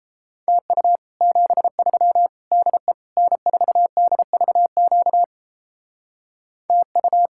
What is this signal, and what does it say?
Morse code, heard as a keyed tone, about 34 wpm, 700 Hz. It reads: TU 73 DE N4DVQ TU
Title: Speed-Racing Common QSO Phrases - 28wpm